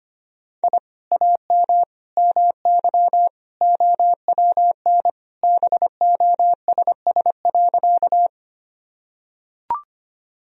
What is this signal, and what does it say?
Morse code, heard as a keyed tone, about 25 wpm, 690 Hz. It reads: I AM MY OWN BOSS.
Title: Sentences from Top 1000 Words - Review Part 4 - 25wpm